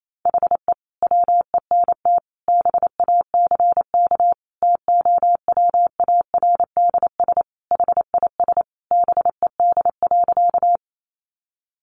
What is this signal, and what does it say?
Morse code, heard as a keyed tone, about 28 wpm, 700 Hz. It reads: HE WENT BACK TOWARDS HIS BED.
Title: Sentences from Top 1100 Words 28wpm